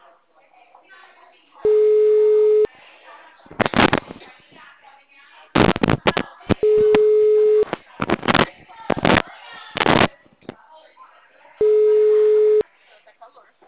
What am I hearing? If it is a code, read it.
Ringback tone (Europe, 425 Hz), with speech.